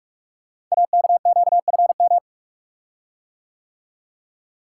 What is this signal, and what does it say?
Morse code, heard as a keyed tone, about 44 wpm, 690 Hz. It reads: AKQFM
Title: Code Groups - 5 Letter:Numbers - 45wpm